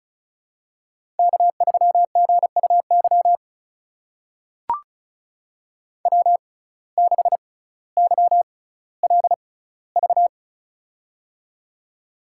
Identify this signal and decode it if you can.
Morse code, heard as a keyed tone, about 36 wpm, spaced as for 18 wpm, 700 Hz. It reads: K3GUY W6YLV
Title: USA Call Signs - 5 Characters - 6x Character Spacing - 35wpm